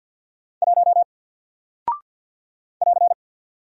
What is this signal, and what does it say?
Morse code, heard as a keyed tone, about 48 wpm, 700 Hz. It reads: 1 <AR>
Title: Single Character (Letter:Number:Punctuation:Prosign) Alt 50wpm